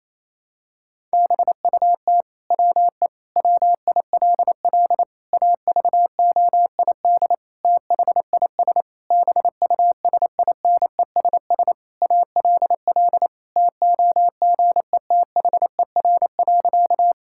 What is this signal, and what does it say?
Morse code, heard as a keyed tone, about 28 wpm, 700 Hz. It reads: BUT WE WILL AVOID THIS BUSINESS ALL TOGETHER.
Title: Sentences from Top 1000 Words - Review Part 6 - 28wpm